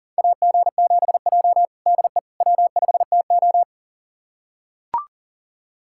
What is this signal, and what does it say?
Morse code, heard as a keyed tone, about 40 wpm, 700 Hz. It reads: AG7J DE W5TO
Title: USA Callsign DE Callsign - 4 Characters - 40wpm